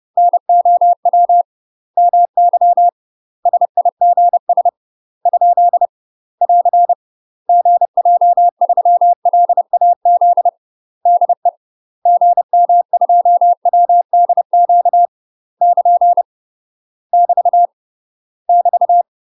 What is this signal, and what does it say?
Morse code, heard as a keyed tone, about 30 wpm, 700 Hz. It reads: NOW MY SIGS ? <AR> GJ3LAZ DE GM2WDQ <KN> <BT> <BT>
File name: UK MASTER QSOs TRAINING / 30WPM / UK MASTER QSOs 0019